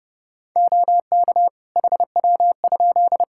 Morse code, heard as a keyed tone, about 30 wpm, 700 Hz. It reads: OK HW?
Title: K1USN SST QSO Practice 30wpm